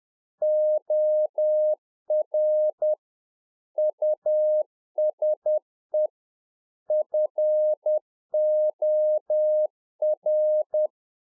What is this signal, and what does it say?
Morse code, heard as a keyed tone, about 10 wpm, 610 Hz. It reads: OR USE FOR
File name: 2011-04-mp3 / Czech Army Morse Keys 10WPM